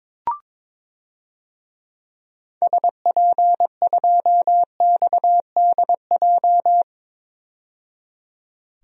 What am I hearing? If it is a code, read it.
Morse code, heard as a keyed tone, about 22 wpm, 700 Hz. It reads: SP2XDJ
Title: DXCC-100 Call Signs - 6 Characters - 22wpm